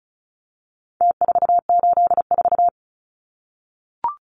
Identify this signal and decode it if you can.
Morse code, heard as a keyed tone, about 34 wpm, 700 Hz. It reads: T484
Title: Numbers - Contest Style - 4-Digit 35wpm